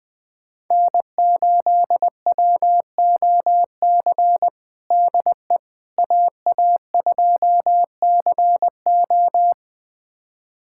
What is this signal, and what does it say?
Morse code, heard as a keyed tone, about 21 wpm, 700 Hz. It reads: N8WOC DE AA2CO
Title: USA Callsign DE Callsign - 5 Characters - 20wpm